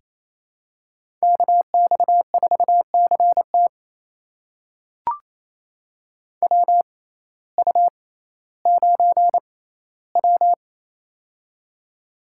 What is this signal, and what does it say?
Morse code, heard as a keyed tone, about 28 wpm, spaced as for 15 wpm, 700 Hz. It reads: KX4CT WU9W
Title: USA Call Signs - 6x Character Spacing - 28wpm